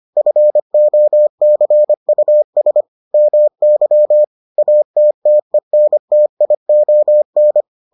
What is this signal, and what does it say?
Morse code, heard as a keyed tone, about 25 wpm, 600 Hz. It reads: FOCUS MY ATTENTION